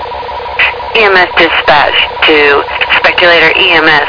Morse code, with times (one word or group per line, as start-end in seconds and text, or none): none